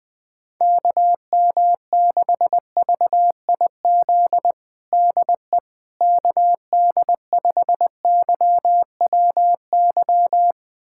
Morse code, heard as a keyed tone, about 20 wpm, 700 Hz. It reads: KM6VIZ DE KD5YWY
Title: USA Callsign DE Callsign - 6 Characters - 20wpm